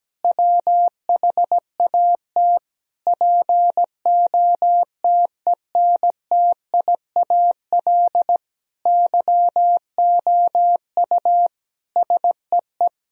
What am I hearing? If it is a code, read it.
Morse code, heard as a keyed tone, about 17 wpm, 700 Hz. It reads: WHAT POTENTIAL YOU SEE
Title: Sentences from Top 1000 Words - Review Part 4 - 17wpm